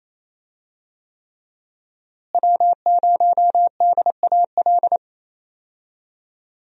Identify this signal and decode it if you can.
Morse code, heard as a keyed tone, about 28 wpm, 700 Hz. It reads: W0DAL
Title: USA Call Signs - 5 Characters - Repeated 3x - 28wpm